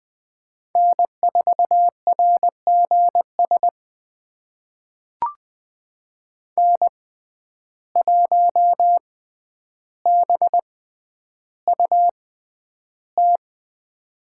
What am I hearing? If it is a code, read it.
Morse code, heard as a keyed tone, about 20 wpm, spaced as for 10 wpm, 700 Hz. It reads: N4RGS N1BUT
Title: USA Call Signs - 5 Characters - 6x Character Spacing - 20wpm